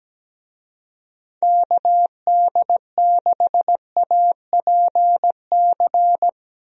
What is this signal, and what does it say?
Morse code, heard as a keyed tone, about 17 wpm, 700 Hz. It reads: KD6APC
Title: USA Call Signs - 6 Characters - Repeated 2x - 17wpm